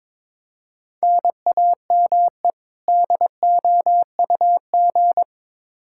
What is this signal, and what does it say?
Morse code, heard as a keyed tone, about 22 wpm, 700 Hz. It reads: NAME DOUG
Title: Practice QSO - Exchange 1 - Beginning - 22wpm